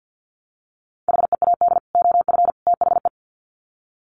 Morse code, heard as a keyed tone, about 50 wpm, 700 Hz. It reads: HEAD OF THE
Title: Sets of 3 Words 50wpm